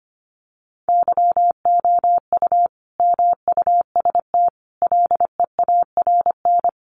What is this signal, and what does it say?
Morse code, heard as a keyed tone, about 25 wpm, 700 Hz. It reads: YOU MUST LEARN
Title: Sentences from Top 900 Words 25wpm